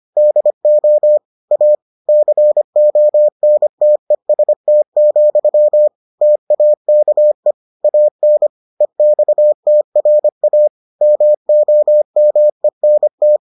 Morse code, heard as a keyed tone, about 25 wpm, 600 Hz. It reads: DO A CONTEST, TAKE AN EXTRA MOMENT